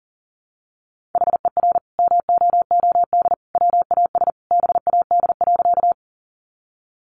Morse code, heard as a keyed tone, about 40 wpm, 700 Hz. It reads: HER MOOD WAS BAD.
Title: Sentences from Top 1100 Words 40wpm